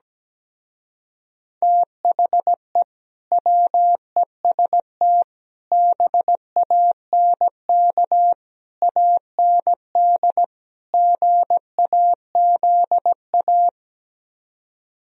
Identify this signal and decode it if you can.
Morse code, heard as a keyed tone, about 17 wpm, 700 Hz. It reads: THE WEST BANK AND GAZA
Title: Speed-Racing - Sets of 5 Words 17wpm